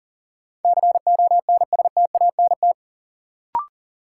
Morse code, heard as a keyed tone, about 40 wpm, 700 Hz. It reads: CONSTANT
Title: Sentences from Top 1500 Words - New Word Drill - 40wpm